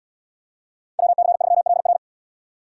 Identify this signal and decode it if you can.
Morse code, heard as a keyed tone, about 64 wpm, 700 Hz. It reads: WC3LR